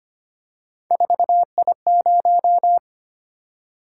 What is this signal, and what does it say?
Morse code, heard as a keyed tone, about 25 wpm, 700 Hz. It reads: 4I0
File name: Three Letter-Number Combinations - 25wpm